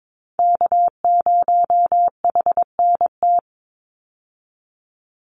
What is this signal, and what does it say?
Morse code, heard as a keyed tone, about 22 wpm, 700 Hz. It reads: K0HNT
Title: USA Call Signs 22wpm